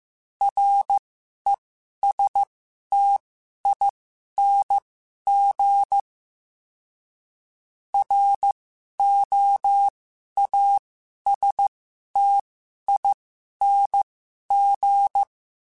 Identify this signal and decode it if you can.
Morse code, heard as a keyed tone, about 15 wpm, spaced as for 11 wpm, 800 Hz. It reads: RESTING ROASTING
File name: Special INGs-mp3s / Special INGs-mp3s / Sngl Ing @10